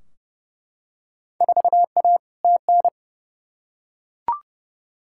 Morse code, heard as a keyed tone, about 31 wpm, 700 Hz. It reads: 4A TN
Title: Field Day Exchanges Encore - 30wpm